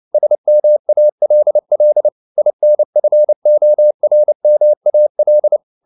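Morse code, heard as a keyed tone, about 29 wpm, 600 Hz. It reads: SMALL INFORMAL
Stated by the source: Morse code at 29 wpm, sending Ken Follett's novel Never in English, from CW med Gnister.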